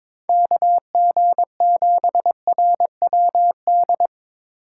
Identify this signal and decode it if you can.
Morse code, heard as a keyed tone, about 22 wpm, 700 Hz. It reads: KG7RWD
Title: USA Call Signs - 4x Character Spacing - 22wpm